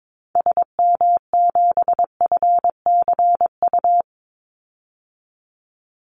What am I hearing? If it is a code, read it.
Morse code, heard as a keyed tone, about 22 wpm, 700 Hz. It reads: SM7FCU
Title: Call Signs Repeated 2x - 22wpm